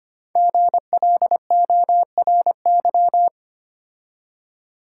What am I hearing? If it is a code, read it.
Morse code, heard as a keyed tone, about 25 wpm, 700 Hz. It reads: GLORY
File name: 5 Letter Words - First 3 Letters Given - 25wpm